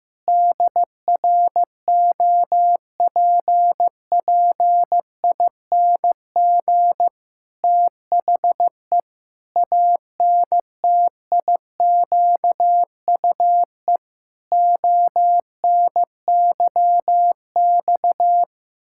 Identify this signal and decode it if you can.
Morse code, heard as a keyed tone, about 15 wpm, 700 Hz. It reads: DROPPING THE ANTIQUE ONYX
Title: Pangram Challenge v2 - 15wpm